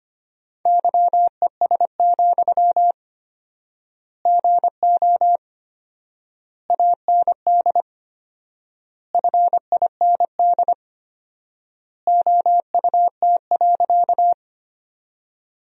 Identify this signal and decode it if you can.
Morse code, heard as a keyed tone, about 25 wpm, 700 Hz. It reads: YES, GO AND FIND OUT.
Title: 4x Word-Spacing - Sentences from Top 500 Words - 25wpm